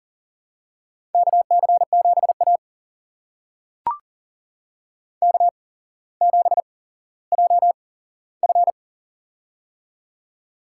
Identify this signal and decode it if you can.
Morse code, heard as a keyed tone, about 39 wpm, spaced as for 17 wpm, 700 Hz. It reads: KC7A K7JF
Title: USA Call Signs - 4 Characters - 8x Character Spacing - 40wpm